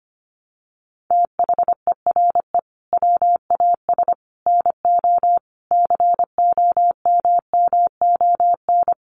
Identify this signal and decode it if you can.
Morse code, heard as a keyed tone, about 25 wpm, 700 Hz. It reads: THERE WAS NO COMMON
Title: Sentences from Top 1200 Words 25wpm